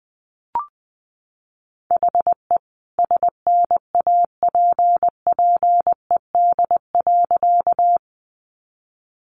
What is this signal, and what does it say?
Morse code, heard as a keyed tone, about 20 wpm, 700 Hz. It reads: HE SNAPPED.
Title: A Sound Of Thunder 20wpm